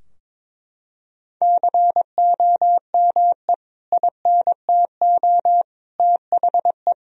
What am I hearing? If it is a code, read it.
Morse code, heard as a keyed tone, about 21 wpm, 700 Hz. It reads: COME INTO THE
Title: Sentences from Top 1000 Words - Review Part 4 - 22wpm